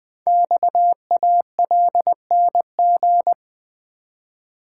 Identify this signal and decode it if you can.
Morse code, heard as a keyed tone, about 20 wpm, 700 Hz. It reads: XALNG
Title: Code Groups - 5 Letters - 20wpm